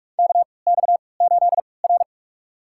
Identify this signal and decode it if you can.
Morse code, heard as a keyed tone, about 46 wpm, spaced as for 26 wpm, 700 Hz. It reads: KX8R
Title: USA Call Signs - 4 Characters - 3x Character Spacing - 45wpm